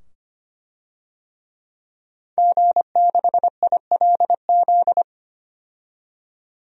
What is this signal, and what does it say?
Morse code, heard as a keyed tone, about 25 wpm, 700 Hz. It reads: G6ILZ
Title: DXCC-100 Call Signs - 5 Characters - 25wpm